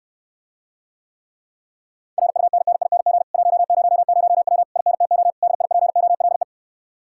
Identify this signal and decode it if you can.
Morse code, heard as a keyed tone, about 69 wpm, 700 Hz. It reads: RUNNING 100W INTO DIPOLE